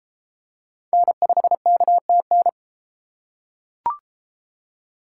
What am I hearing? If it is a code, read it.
Morse code, heard as a keyed tone, about 31 wpm, 700 Hz. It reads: N5KTN